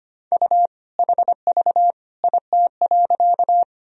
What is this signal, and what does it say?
Morse code, heard as a keyed tone, about 25 wpm, 700 Hz. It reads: U HV IT.
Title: Sentences from Top 100 Words and Abbreviations 25wpm